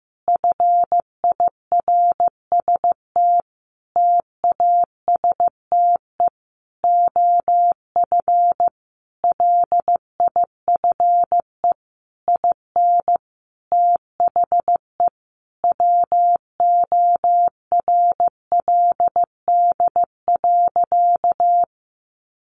Morse code, heard as a keyed tone, about 15 wpm, 700 Hz. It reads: FIRST TASTE OF LIFE IN THE WORLD.
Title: Sentences from Top 1100 Words 15wpm